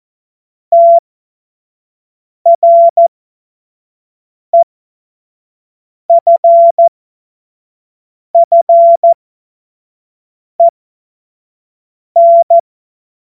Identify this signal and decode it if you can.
Morse code, heard as a keyed tone, about 14 wpm, spaced as for 5 wpm, 680 Hz. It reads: TREFFEN
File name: Morsekurs / Witze / Witz 03. Mai 2022 25 F70